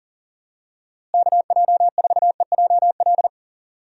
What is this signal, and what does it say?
Morse code, heard as a keyed tone, about 37 wpm, 700 Hz. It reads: KJ4EJL
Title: USA Call Signs - 6 Characters - Repeated 2x - 40wpm